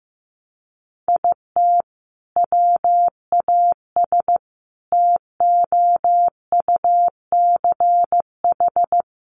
Morse code, heard as a keyed tone, about 15 wpm, 700 Hz. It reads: IT WAS TOUCH
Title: Sentences from Top 1200 Words 15wpm